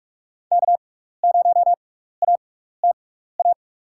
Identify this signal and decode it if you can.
Morse code, heard as a keyed tone, about 44 wpm, spaced as for 15 wpm, 700 Hz. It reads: K0ATA
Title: USA Call Signs - 6x Character Spacing - 45wpm